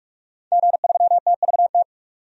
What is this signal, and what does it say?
Morse code, heard as a keyed tone, about 47 wpm, 700 Hz. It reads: G3TVT
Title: DXCC-100 Call Signs - 5 Characters - 45wpm